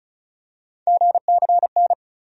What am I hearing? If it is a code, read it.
Morse code, heard as a keyed tone, about 34 wpm, 700 Hz. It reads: GCN
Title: USA Call Sign Postfixes - 3 Letters 35wpm